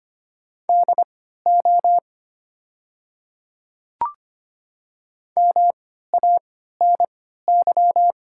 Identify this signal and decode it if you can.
Morse code, heard as a keyed tone, about 25 wpm, spaced as for 14 wpm, 700 Hz. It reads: DO MANY
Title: 3x Character Spacing - Top 300 Words - 25wpm